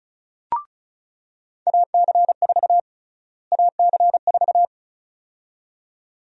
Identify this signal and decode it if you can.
Morse code, heard as a keyed tone, about 36 wpm, 700 Hz. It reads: AC4 AC4
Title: USA Call Sign Prefixes - Repeated 2x - 35wpm